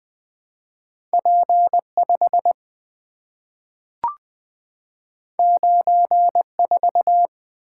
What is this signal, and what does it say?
Morse code, heard as a keyed tone, about 20 wpm, 700 Hz. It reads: P5 94
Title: Two Letter-Number Combinations - 20wpm